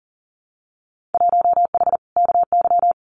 Morse code, heard as a keyed tone, about 38 wpm, 700 Hz. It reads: 1H KY
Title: Winter Field Day Exchanges 40wpm